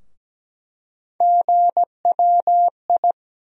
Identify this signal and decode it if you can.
Morse code, heard as a keyed tone, about 17 wpm, 700 Hz. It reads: GWI
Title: USA Call Sign Postfixes - 3 Letters 17wpm